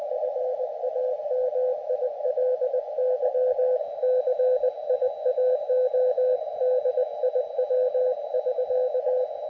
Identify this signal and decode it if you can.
Morse code, heard as a keyed tone, about 20 wpm, 510 Hz, with radio noise. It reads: FAMILY CIAO DIW <SK>
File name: Expedition / SWL UT5NM 20.12.2012 VANGUARD UA1CEG p IT9IFI